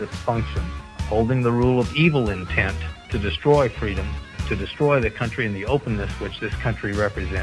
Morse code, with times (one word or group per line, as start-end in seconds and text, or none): none